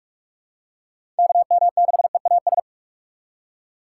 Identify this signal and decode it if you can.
Morse code, heard as a keyed tone, about 43 wpm, 700 Hz. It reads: KM6EAS
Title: USA Call Signs - 3x Character Spacing - 45wpm